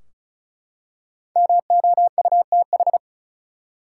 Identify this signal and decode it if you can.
Morse code, heard as a keyed tone, about 34 wpm, 700 Hz. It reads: MOUTH
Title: Five Letter Words - 35wpm